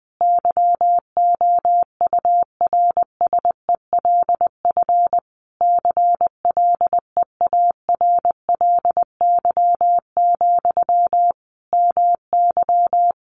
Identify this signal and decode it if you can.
Morse code, heard as a keyed tone, about 20 wpm, 700 Hz. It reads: YOURSELF CLEARLY, MY
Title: Sentences from Top 1300 Words 20wpm